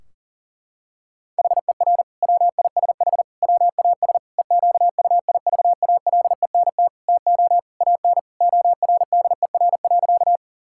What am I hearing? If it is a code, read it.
Morse code, heard as a keyed tone, about 40 wpm, 700 Hz. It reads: HER WISH WAS EQUIVALENT TO AN ORDER.
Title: Sentences from Top 1600 Words 40wpm